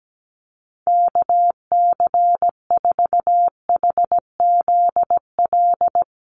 Morse code, heard as a keyed tone, about 17 wpm, 700 Hz. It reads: KC4HZL
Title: USA Call Signs Repeated 3x 17wpm